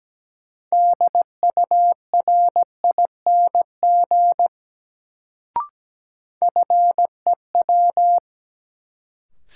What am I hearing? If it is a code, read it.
Morse code, heard as a keyed tone, about 17 wpm, 700 Hz. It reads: DURING FEW
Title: Sets of 2 Words - New Word Drill - 17wpm